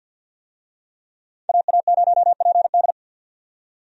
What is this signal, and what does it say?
Morse code, heard as a keyed tone, about 46 wpm, 700 Hz. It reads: AA0PD
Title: USA Call Signs - 5 Characters - 50wpm